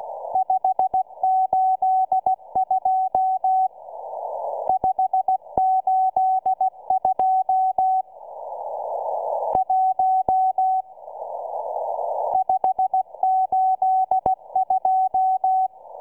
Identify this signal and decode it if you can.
Morse code, heard as a keyed tone, about 16 wpm, 750 Hz. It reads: 582 582 1 582